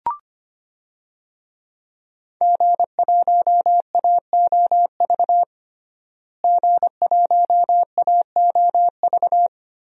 Morse code, heard as a keyed tone, about 25 wpm, 700 Hz. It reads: G1AOV G1AOV
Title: DXCC-100 Call Signs - 5 Characters - Repeated 3x - 25wpm